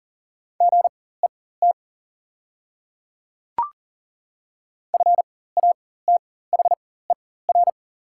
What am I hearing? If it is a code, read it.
Morse code, heard as a keyed tone, about 41 wpm, spaced as for 19 wpm, 700 Hz. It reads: GET FATHER
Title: 4x Character Spacing - Top 200 Words - 40wpm